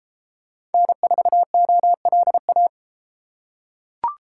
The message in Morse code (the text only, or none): N4OLA